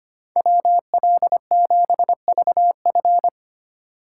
Morse code, heard as a keyed tone, about 25 wpm, 700 Hz. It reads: WL7VF
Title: USA Call Signs - 5 Characters - 2x Character Spacing - 25wpm